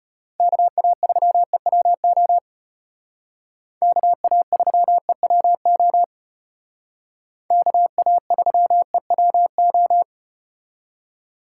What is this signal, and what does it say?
Morse code, heard as a keyed tone, about 31 wpm, 700 Hz. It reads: KA3EWO KA3EWO KA3EWO
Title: Speed Racing - USA 2x3 Call Signs 25wpm